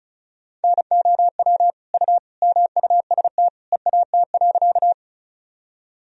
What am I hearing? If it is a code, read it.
Morse code, heard as a keyed tone, about 36 wpm, 700 Hz. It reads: NOW U MUST EAT.
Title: Sentences from Top 300 Words and Abbreviations 35wpm